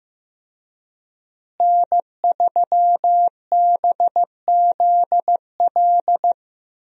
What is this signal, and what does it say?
Morse code, heard as a keyed tone, about 15 wpm, 700 Hz. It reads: N3BZL
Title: USA Call Signs - 5 Characters - Repeated 3x - 15wpm